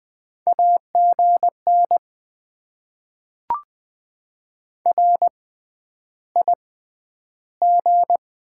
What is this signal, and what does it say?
Morse code, heard as a keyed tone, about 20 wpm, 700 Hz. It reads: A G N R I G
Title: Top 20 QSO Elements - 6x Character Spacing - 20wpm